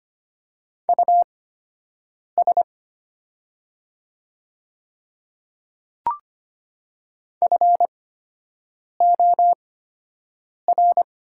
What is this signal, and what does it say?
Morse code, heard as a keyed tone, about 25 wpm, spaced as for 7 wpm, 700 Hz. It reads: US FOR
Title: 8x Character Spacing - Top 300 Words - 25wpm